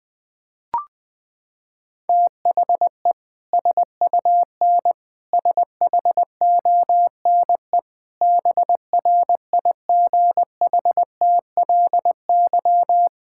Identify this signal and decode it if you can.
Morse code, heard as a keyed tone, about 20 wpm, 700 Hz. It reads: THE SUN SHONE BRIGHTLY
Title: Sentences with 2-Word Phrases - 20wpm